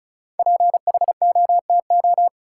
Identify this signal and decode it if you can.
Morse code, heard as a keyed tone, about 34 wpm, 700 Hz. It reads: PHOTO
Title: Sentences from Top 1200 Words - New Word Drill - 35wpm